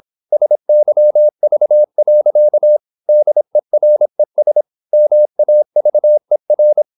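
Morse code, heard as a keyed tone, about 26 wpm, 600 Hz. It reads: SYV. DERES MAVER
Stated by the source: High-speed (QRQ) Morse at 26 wpm.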